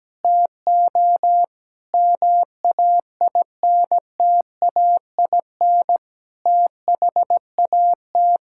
Morse code, heard as a keyed tone, about 17 wpm, 700 Hz. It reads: TO MAINTAIN THAT
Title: Sentences from Top 1000 Words - Review Part 5 - 17wpm